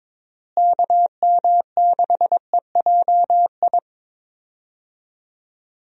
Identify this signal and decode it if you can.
Morse code, heard as a keyed tone, about 22 wpm, 700 Hz. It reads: KM6EJI
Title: USA Call Signs - 6 Characters - Repeated 3x - 22wpm